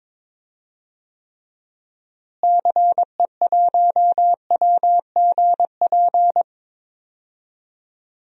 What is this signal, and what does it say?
Morse code, heard as a keyed tone, about 22 wpm, 700 Hz. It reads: CE1WGP
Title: DXCC-100 Call Signs - 6 Characters - Repeated 3x - 22wpm